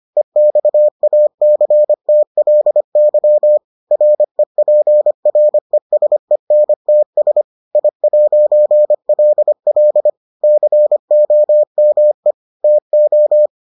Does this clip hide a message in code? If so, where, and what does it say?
Yes: Morse code from 0.2 to 13.5 s, heard as a keyed tone, about 25 wpm, 600 Hz. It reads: EXACTLY REPRESENTS I'LL COME TO